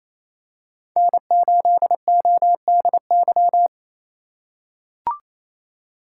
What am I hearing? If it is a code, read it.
Morse code, heard as a keyed tone, about 28 wpm, 700 Hz. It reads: N8ODY
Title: USA Call Signs - 5 Characters - 4x Character Spacing - 28wpm